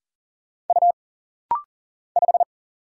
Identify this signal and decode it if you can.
Morse code, heard as a keyed tone, about 40 wpm, 700 Hz. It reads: U 5
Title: Single Letter-Number Alt - Rapid-Fire 40wpm